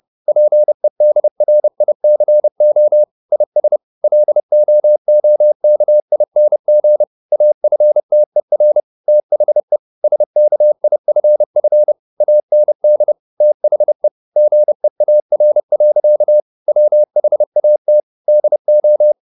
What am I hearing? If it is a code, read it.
Morse code, heard as a keyed tone, about 30 wpm, 600 Hz. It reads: PEDRICO IS LOOKING AFTER THE SKIFF AND THE GEAR. WHAT DO